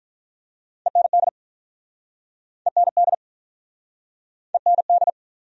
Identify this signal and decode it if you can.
Morse code, heard as a keyed tone, about 53 wpm, 700 Hz. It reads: END END END